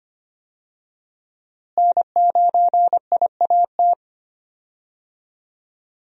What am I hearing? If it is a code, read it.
Morse code, heard as a keyed tone, about 25 wpm, 700 Hz. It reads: N9IAT
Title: USA Call Signs - 5 Characters - Repeated 2x - 25wpm